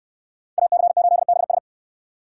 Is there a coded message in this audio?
Yes: Morse code from 0.6 to 1.6 s, heard as a keyed tone, about 64 wpm, 700 Hz. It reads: AC8BS